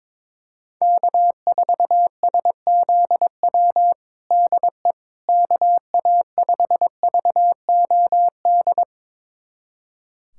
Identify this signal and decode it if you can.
Morse code, heard as a keyed tone, about 21 wpm, 700 Hz. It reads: K4SZW DE KA5VOD
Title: USA Callsign DE Callsign 22wpm